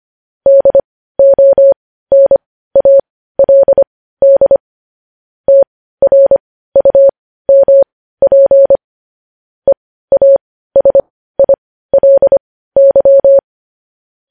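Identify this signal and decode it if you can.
Morse code, heard as a keyed tone, about 25 wpm, spaced as for 15 wpm, 550 Hz. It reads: DONALD TRUMP EASILY
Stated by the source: Morse news headlines practice at 15.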